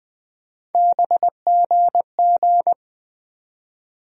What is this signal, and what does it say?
Morse code, heard as a keyed tone, about 20 wpm, 700 Hz. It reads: BGG
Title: USA Call Sign Postfixes - 3 Letters 20wpm